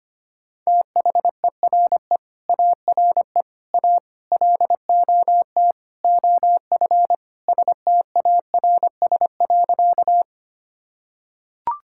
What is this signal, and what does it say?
Morse code, heard as a keyed tone, about 25 wpm, 700 Hz. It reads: THERE ARE A LOT OF STARS.
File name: practice / 25wpm / Sentences with 4-Word Phrases - 25wpm